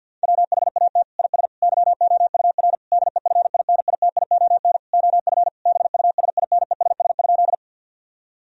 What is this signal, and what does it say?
Morse code, heard as a keyed tone, about 50 wpm, 700 Hz. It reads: WHAT IS YOUR DEFINITION OF BUSINESS?
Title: Sentences from Top 1000 Words - Review Part 4 - 50wpm